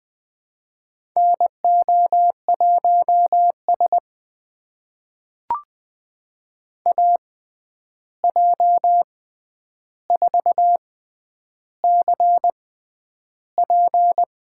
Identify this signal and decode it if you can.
Morse code, heard as a keyed tone, about 20 wpm, spaced as for 10 wpm, 700 Hz. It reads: NO1S AJ4CP
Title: USA Call Signs - 6x Character Spacing - 20wpm